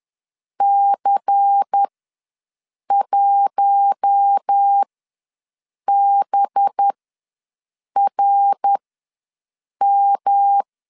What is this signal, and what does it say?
Morse code, heard as a keyed tone, about 10 wpm, spaced as for 6 wpm, 800 Hz. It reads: C1BRM
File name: LEZIONI / 5 LBVC / 5 LBVC ws 10wpm 3733